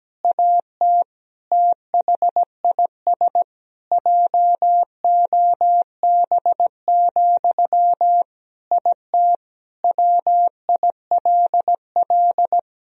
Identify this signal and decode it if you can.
Morse code, heard as a keyed tone, about 17 wpm, 700 Hz. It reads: AT THIS JOB, IT WILL
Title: Sentences from Top 1000 Words - Review Part 4 - 17wpm